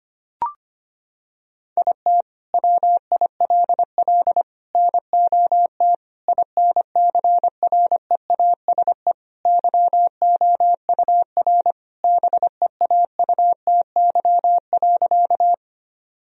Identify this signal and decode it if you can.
Morse code, heard as a keyed tone, about 25 wpm, 700 Hz. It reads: IT WILL NOT INCREASE YOUR BEAUTY.
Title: Sentences from Top 1000 Words - Review Part 4 - 25wpm